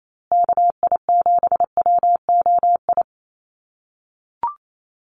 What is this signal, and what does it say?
Morse code, heard as a keyed tone, about 28 wpm, 700 Hz. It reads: KI7WOI
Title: USA Call Signs Repeated 2x 28wpm